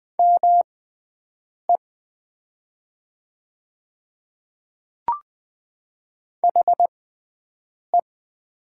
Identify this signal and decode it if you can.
Morse code, heard as a keyed tone, about 20 wpm, spaced as for 7 wpm, 700 Hz. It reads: ME HE
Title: 6x Character Spacing - Top 200 Words - 20wpm